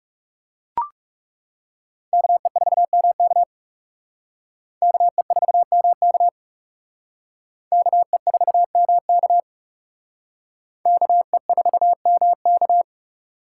Morse code, heard as a keyed tone, about 37 wpm, 700 Hz. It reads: KE4MK KE4MK KE4MK KE4MK
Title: Speed Racing - USA 2x2 Call Signs 30wpm